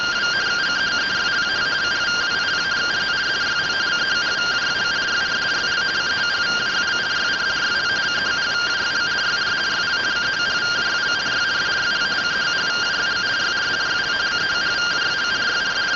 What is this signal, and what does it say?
Two-tone teleprinter signal, about 170 Hz shift.